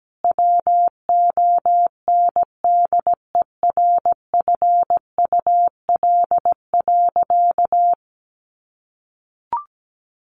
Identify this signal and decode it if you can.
Morse code, heard as a keyed tone, about 17 wpm, 700 Hz. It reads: WONDERFUL.
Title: Sentences from Top 1100 Words 17wpm